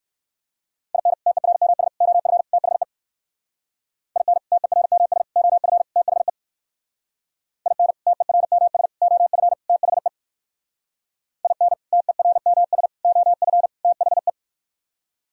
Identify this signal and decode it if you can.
Morse code, heard as a keyed tone, about 59 wpm, 700 Hz. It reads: IN TERMS OF THE IN TERMS OF THE IN TERMS OF THE IN TERMS OF THE